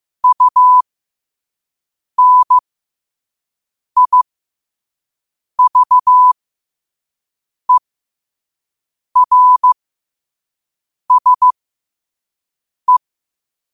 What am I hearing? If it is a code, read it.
Morse code, heard as a keyed tone, about 15 wpm, spaced as for 5 wpm, 1000 Hz. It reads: UNIVERSE
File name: qotdfiles / 5wpm